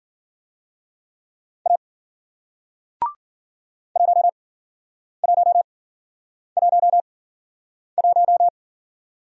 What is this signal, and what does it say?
Morse code, heard as a keyed tone, about 52 wpm, spaced as for 9 wpm, 700 Hz. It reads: A 1111